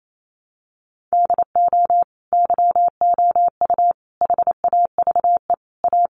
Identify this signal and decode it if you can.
Morse code, heard as a keyed tone, about 28 wpm, 700 Hz. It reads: DO YOU HAVE A
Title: Sentences from Top 1400 Words 28wpm